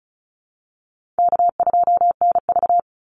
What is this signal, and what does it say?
Morse code, heard as a keyed tone, about 34 wpm, 700 Hz. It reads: K2NV
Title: Easy Call Signs Repeated 2x - 35wpm